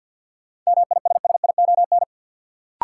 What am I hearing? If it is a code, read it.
Morse code, heard as a keyed tone, about 51 wpm, 700 Hz. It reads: MISSION
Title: Sentences from Top 1600 Words - New Word Drill - 50wpm